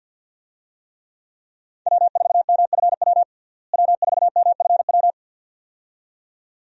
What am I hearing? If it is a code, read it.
Morse code, heard as a keyed tone, about 50 wpm, 700 Hz. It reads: W4MFW W4MFW
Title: USA Call Signs - 5 Characters - Repeated 2x - 50wpm